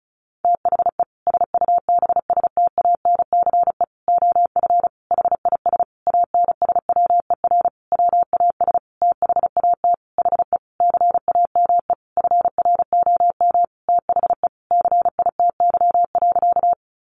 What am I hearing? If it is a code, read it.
Morse code, heard as a keyed tone, about 34 wpm, 700 Hz. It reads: THE SUBSTANCE OF HIS ANSWER WAS THAT HE CAME FROM THE CITY.
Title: Sentences from Top 1500 Words 35wpm